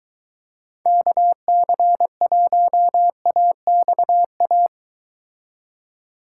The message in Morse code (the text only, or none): KC1AXA